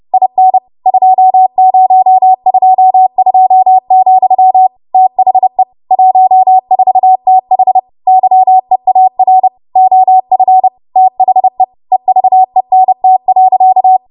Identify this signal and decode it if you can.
Morse code, heard as a keyed tone, about 30 wpm, 750 Hz. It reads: IN 2022, THE 14TH YEAR OF THE EVENT.
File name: Morse / 30WPM